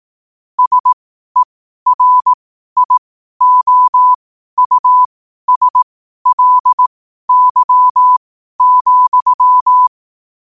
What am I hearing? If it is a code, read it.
Morse code, heard as a keyed tone, about 18 wpm, spaced as for 12 wpm, 1000 Hz. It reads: SERIOUSLY,
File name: qotdfiles / 12wpm